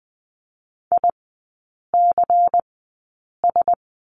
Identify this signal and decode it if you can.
Morse code, heard as a keyed tone, about 20 wpm, spaced as for 8 wpm, 690 Hz. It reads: ICS
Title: 093 Lesson 16e - Letter c - Three Characters 20wpm